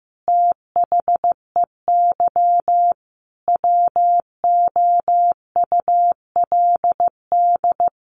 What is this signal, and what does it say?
Morse code, heard as a keyed tone, about 15 wpm, 700 Hz. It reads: THEY WOULD
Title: Sentences from Top 1400 Words 15wpm